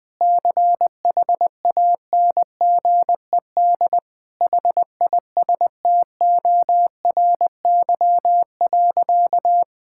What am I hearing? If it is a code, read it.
Morse code, heard as a keyed tone, about 20 wpm, 700 Hz. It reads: CHANGED HISTORY.
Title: Sentences with 7-Word Phrases - Spoken Beginning - 20wpm